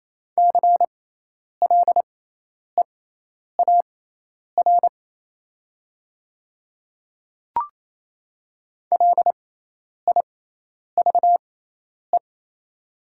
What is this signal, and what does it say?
Morse code, heard as a keyed tone, about 28 wpm, spaced as for 10 wpm, 700 Hz. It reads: CLEAR LIVE
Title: 6x Character Spacing - Top 500 Words - 28wpm